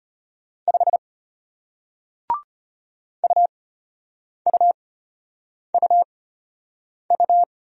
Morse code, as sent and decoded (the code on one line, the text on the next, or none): ..... / ..- ..- ..- ..-
5 UUUU